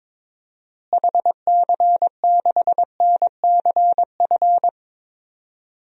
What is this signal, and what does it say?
Morse code, heard as a keyed tone, about 21 wpm, 700 Hz. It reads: HC6NCF
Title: DXCC-100 Call Signs - 6 Characters - Repeated 3x - 22wpm